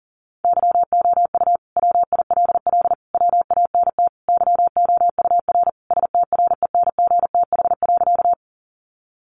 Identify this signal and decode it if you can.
Morse code, heard as a keyed tone, about 40 wpm, 700 Hz. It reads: YOU WILL WANT YOUR STRENGTH.
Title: Sentences from Top 1300 Words 40wpm